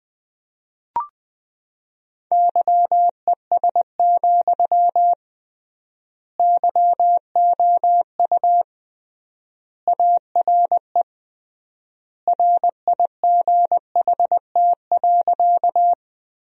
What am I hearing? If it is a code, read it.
Morse code, heard as a keyed tone, about 20 wpm, 700 Hz. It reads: YES, YOU ARE RIGHT.
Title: 3x Word-Spacing - Sentences from Top 500 Words - 20wpm